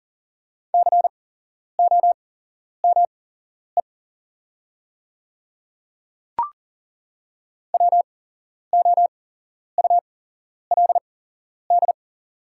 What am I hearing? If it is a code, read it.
Morse code, heard as a keyed tone, about 40 wpm, spaced as for 11 wpm, 690 Hz. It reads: COME WOULD
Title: 8x Character Spacing - Top 100 Words - 40wpm